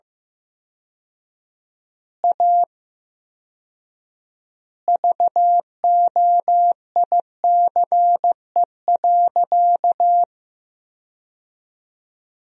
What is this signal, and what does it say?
Morse code, heard as a keyed tone, about 15 wpm, 700 Hz. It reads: A VOICE.
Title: 4x Word-Spacing - Sentences from Top 500 Words - 15wpm